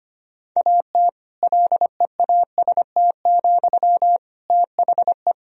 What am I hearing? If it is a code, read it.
Morse code, heard as a keyed tone, about 25 wpm, 700 Hz. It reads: AT LEAST, THE
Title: Sentences from Top 1000 Words - Review Part 4 - 25wpm